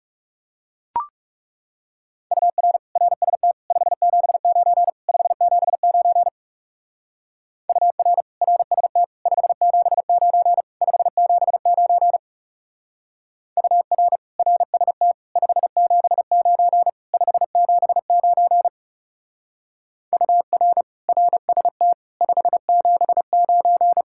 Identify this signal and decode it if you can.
Morse code, heard as a keyed tone, about 37 wpm, 700 Hz. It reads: UR RST 579 579 UR RST 579 579 UR RST 579 579 UR RST 579
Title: Speed-Racing Common QSO Phrases Encore - 30wpm